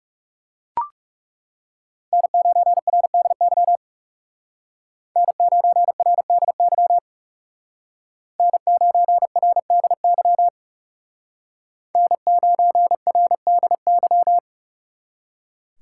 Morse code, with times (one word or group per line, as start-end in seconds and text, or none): 2.1-3.8 N9RDY
5.2-7.0 N9RDY
8.4-10.5 N9RDY
12.0-14.4 N9RDY